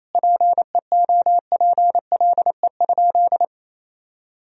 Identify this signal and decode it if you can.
Morse code, heard as a keyed tone, about 28 wpm, 700 Hz. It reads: PEOPLE?
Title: Sentences from Top 1000 Words - Review Part 1 - 28wpm